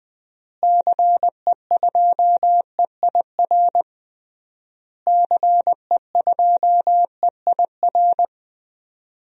Morse code, heard as a keyed tone, about 20 wpm, 700 Hz. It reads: CE2EIR CE2EIR
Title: DXCC-100 Call Signs - 6 Characters - Repeated 3x - 20wpm